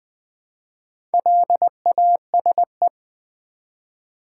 Morse code, heard as a keyed tone, about 20 wpm, 700 Hz. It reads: LASE
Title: Build A Word - 1 Letter At A Time - 3 to 5-Letter Words - 20wpm